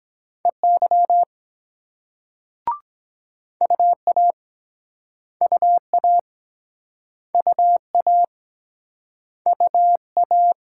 Morse code, heard as a keyed tone, about 23 wpm, 700 Hz. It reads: EY UA UA UA UA